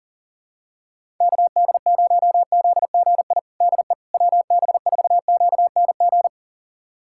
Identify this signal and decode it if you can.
Morse code, heard as a keyed tone, about 40 wpm, 700 Hz. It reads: KD0ZGI DE WB4QNG
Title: USA Callsign DE Callsign - 6 Characters - 40wpm